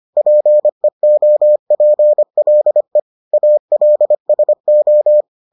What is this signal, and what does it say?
Morse code, heard as a keyed tone, about 25 wpm, 600 Hz. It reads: PEOPLE ALSO